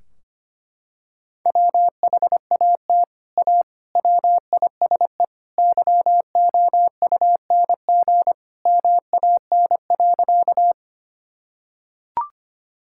Morse code, heard as a keyed tone, about 25 wpm, 700 Hz. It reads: WHAT A WISE YOUNG MAN.
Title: Sentences from Top 1600 Words 25wpm